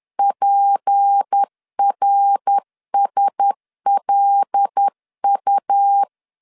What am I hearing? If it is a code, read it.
Morse code, heard as a keyed tone, about 10 wpm, 800 Hz. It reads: PRSLU